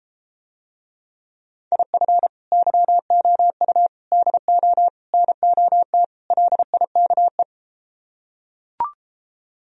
Morse code, heard as a keyed tone, about 33 wpm, 700 Hz. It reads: IF YOU DO NOT LIKE